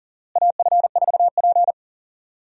Morse code, heard as a keyed tone, about 40 wpm, 700 Hz. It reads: AF4P
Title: USA Call Signs - 4 Characters - Repeated 3x - 40wpm